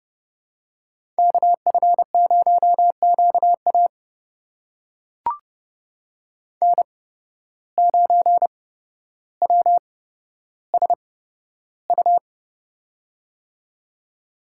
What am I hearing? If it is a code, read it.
Morse code, heard as a keyed tone, about 30 wpm, spaced as for 13 wpm, 700 Hz. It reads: KF0QA N9WSU
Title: USA Call Signs - 5 Characters - 8x Character Spacing - 30wpm